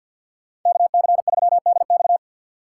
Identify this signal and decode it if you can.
Morse code, heard as a keyed tone, about 47 wpm, 700 Hz. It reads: KC3DX
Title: USA Call Signs - 4x Character Spacing - 50wpm